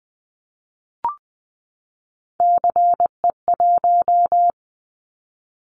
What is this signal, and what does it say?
Morse code, heard as a keyed tone, about 20 wpm, 700 Hz. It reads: CE1
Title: Callsign Prefixes for Top 100 DXCC - 20wpm